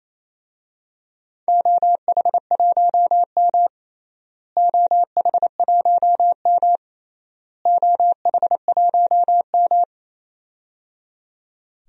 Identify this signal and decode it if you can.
Morse code, heard as a keyed tone, about 28 wpm, 700 Hz. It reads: OH1M OH1M OH1M
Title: DXCC-100 Call Signs - 4 Characters - Repeated 3x - 28wpm